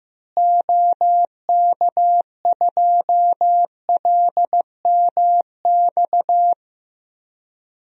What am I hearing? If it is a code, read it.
Morse code, heard as a keyed tone, about 15 wpm, 700 Hz. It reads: OK2LMX
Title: DXCC-100 Call Signs - 6 Characters - Repeated 3x - 15wpm